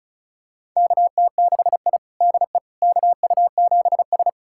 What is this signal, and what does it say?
Morse code, heard as a keyed tone, about 34 wpm, 690 Hz. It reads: KT6I DE KU7S
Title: USA Callsign DE Callsign - 4 Characters - 35wpm